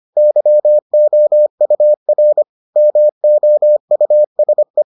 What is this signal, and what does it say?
Morse code, heard as a keyed tone, about 25 wpm, 600 Hz. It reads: YOUR MOUSE